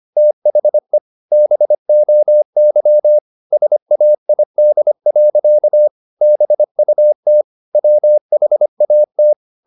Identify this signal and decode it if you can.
Morse code, heard as a keyed tone, about 25 wpm, 600 Hz. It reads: THE BOY SAID. BUT WHAT
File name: the old man and the sea part 1 25wpm